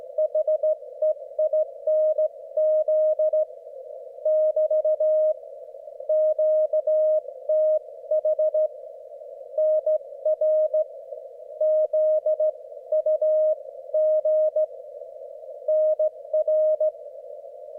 Morse code, heard as a keyed tone, about 14 wpm, 620 Hz. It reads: HEINZ <BT> QTH NR ZUG NR
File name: Morsekurs / Witze / QSO W4 HB9DAL und HB9BPH